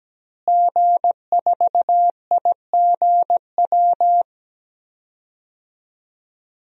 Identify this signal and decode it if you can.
Morse code, heard as a keyed tone, about 17 wpm, 700 Hz. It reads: G4IGW
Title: DXCC-100 Call Signs - 5 Characters - Repeated 2x - 17wpm